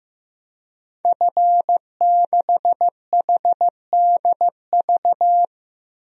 Morse code, heard as a keyed tone, about 15 wpm, 700 Hz. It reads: F6HDV
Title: DXCC-100 Call Signs - 5 Characters - 15wpm